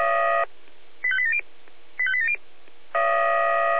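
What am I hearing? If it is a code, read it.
Five-tone selective call (ZVEI), heard as quick runs of five tones, tones of about 70 ms. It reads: 85870 85870
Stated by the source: ZVEI five-tone selective call received on SDR.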